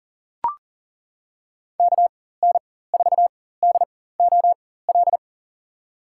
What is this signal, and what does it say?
Morse code, heard as a keyed tone, about 40 wpm, spaced as for 19 wpm, 700 Hz. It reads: KN4DOL
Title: USA Call Signs - 4x Character Spacing - 40wpm